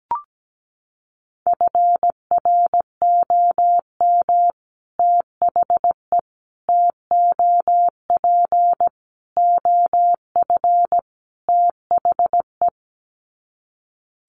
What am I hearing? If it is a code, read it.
Morse code, heard as a keyed tone, about 17 wpm, 700 Hz. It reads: FROM THE TOP OF THE
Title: Sets of 5 Words Encore 17wpm